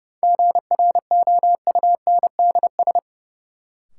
Morse code, heard as a keyed tone, about 31 wpm, 700 Hz. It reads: GROUNDS
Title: Sentences from Top 1300 Words - New Word Drill Alt - 30wpm